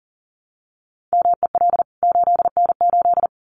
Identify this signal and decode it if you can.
Morse code, heard as a keyed tone, about 40 wpm, 700 Hz. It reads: MEL 8N8
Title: CWT Exchange 40wpm